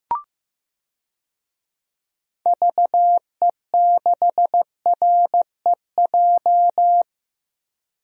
Morse code, heard as a keyed tone, about 16 wpm, 700 Hz. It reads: VE6REJ
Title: DXCC-100 Call Signs - 6 Characters - Repeated 3x - 15wpm